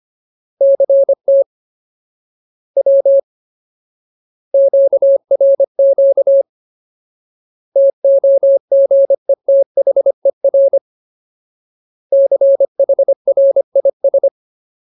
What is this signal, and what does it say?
Morse code, heard as a keyed tone, about 25 wpm, 550 Hz. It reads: CT W QRQ TOGETHER CHRIS
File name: Ultimate Word List / 25WPM Ultimate